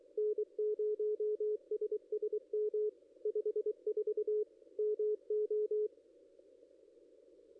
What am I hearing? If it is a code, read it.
Morse code, heard as a keyed tone, about 23 wpm, 420 Hz, with radio noise. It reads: N0SSM 54 MO
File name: contests / ss-cw-k5tr-op-n5rz -1600